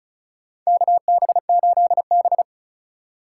Morse code, heard as a keyed tone, about 35 wpm, 700 Hz. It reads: KB8B
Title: USA Call Signs - 4 Characters - 2x Character Spacing - 35wpm